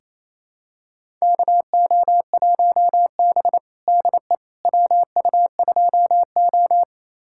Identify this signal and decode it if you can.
Morse code, heard as a keyed tone, about 28 wpm, 690 Hz. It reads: KO1B DE WU2O
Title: USA Callsign DE Callsign - 4 Characters - 28wpm